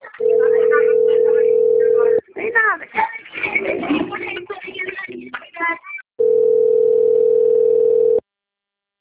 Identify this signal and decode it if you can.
Ringback tone (North America), 440 + 480 Hz, with speech.
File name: rg-1600-unknown- - - .62339